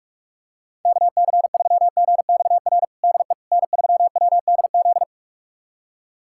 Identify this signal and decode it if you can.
Morse code, heard as a keyed tone, about 43 wpm, 700 Hz. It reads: KC3GXR DE N3WDZ
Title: USA Callsign DE Callsign 45wpm